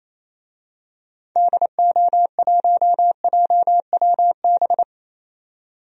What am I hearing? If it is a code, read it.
Morse code, heard as a keyed tone, about 28 wpm, 700 Hz. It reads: DO1JWB
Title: DXCC-100 Call Signs - 6 Characters - 28wpm